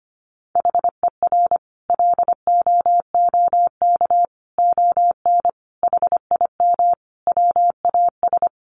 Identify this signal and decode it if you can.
Morse code, heard as a keyed tone, about 25 wpm, 700 Hz. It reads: HER LOOK ON HIM WAS
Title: Sentences from Top 1400 Words 25wpm